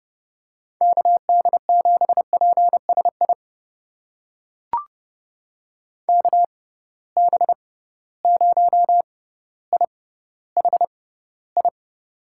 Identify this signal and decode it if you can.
Morse code, heard as a keyed tone, about 32 wpm, spaced as for 16 wpm, 700 Hz. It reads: KD7PSI KB0IHI
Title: USA Call Signs - 6 Characters - 6x Character Spacing - 30wpm